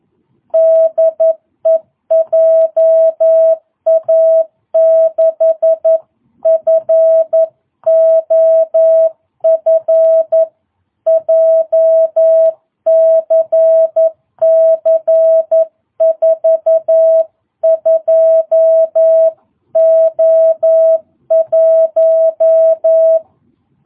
Morse code, heard as a keyed tone, about 11 wpm, 660 Hz. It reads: DE JA6FOF JCC 42O1